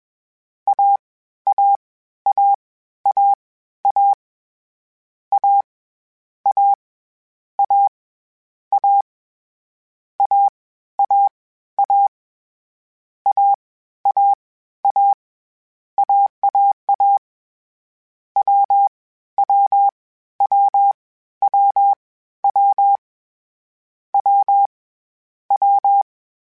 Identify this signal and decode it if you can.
Morse code, heard as a keyed tone, about 21 wpm, 800 Hz. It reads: A A A A A A A A A A A A A A A AAA W W W W W W W
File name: Difficult Combos / Difficult Combos-mp3s / AWJ1 @20